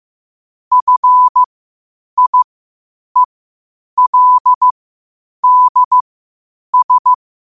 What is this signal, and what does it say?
Morse code, heard as a keyed tone, about 15 wpm, spaced as for 8 wpm, 1000 Hz. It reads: FIELDS
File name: qotdfiles / 8wpm